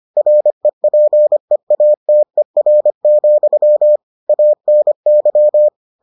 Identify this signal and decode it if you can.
Morse code, heard as a keyed tone, about 25 wpm, 600 Hz. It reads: REPEATER, ANY